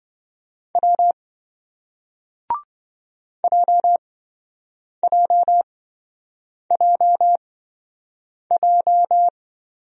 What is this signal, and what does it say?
Morse code, heard as a keyed tone, about 27 wpm, spaced as for 7 wpm, 700 Hz. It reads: W JJJJ